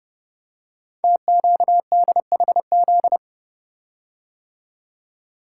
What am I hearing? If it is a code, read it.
Morse code, heard as a keyed tone, about 28 wpm, 700 Hz. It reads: TQDHZ
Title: Code Groups - 5 Letter:Numbers - 30wpm